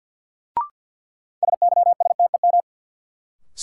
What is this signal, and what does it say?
Morse code, heard as a keyed tone, about 49 wpm, 700 Hz. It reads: SYSTEM
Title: Sentences from Top 1000 Words - Review Part 3 - New Word Drill - 50wpm